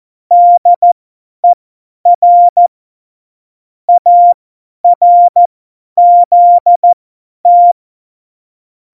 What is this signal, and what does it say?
Morse code, heard as a keyed tone, about 14 wpm, spaced as for 10 wpm, 700 Hz. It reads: DER ARZT
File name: Morsekurs / Witze / Witz 23. Januar 2018, 50 F70